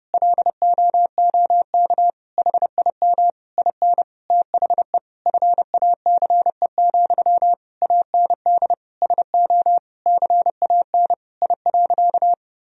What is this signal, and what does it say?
Morse code, heard as a keyed tone, about 29 wpm, 700 Hz. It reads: LOOK HIM IN THE FACE, AND SO CAN I.
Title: Sentences from Top 1000 Words - Review Part 2 - 30wpm